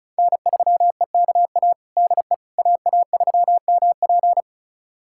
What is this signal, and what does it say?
Morse code, heard as a keyed tone, about 36 wpm, 700 Hz. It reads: N3EKA DE AA3MP
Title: USA Callsign DE Callsign - 5 Characters - 35wpm